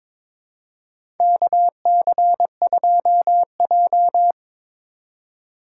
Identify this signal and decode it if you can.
Morse code, heard as a keyed tone, about 22 wpm, 700 Hz. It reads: KC2J
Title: USA Call Signs - 4 Characters - 6x Character Spacing - 22wpm